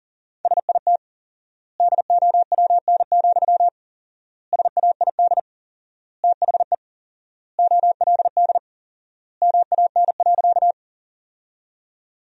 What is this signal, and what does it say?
Morse code, heard as a keyed tone, about 40 wpm, 700 Hz. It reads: SIT DOWN, SAID THE OLD MAN.
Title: 4x Word-Spacing - Sentences from Top 500 Words - 40wpm